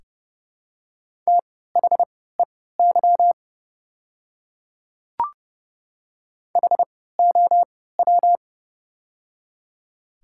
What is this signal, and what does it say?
Morse code, heard as a keyed tone, about 30 wpm, spaced as for 17 wpm, 700 Hz. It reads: THEY HOW
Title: 3x Character Spacing - Top 300 Words - 30wpm